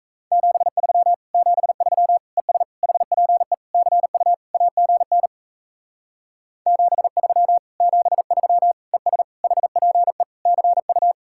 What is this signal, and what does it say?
Morse code, heard as a keyed tone, about 36 wpm, 700 Hz. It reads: 73 73 ES HPE CU AGN 73 73 ES HPE CU